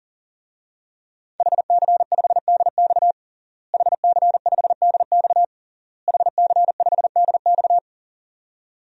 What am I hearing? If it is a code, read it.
Morse code, heard as a keyed tone, about 40 wpm, 700 Hz. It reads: HC5DX HC5DX HC5DX
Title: DXCC-100 Call Signs - 5 Characters - Repeated 3x - 40wpm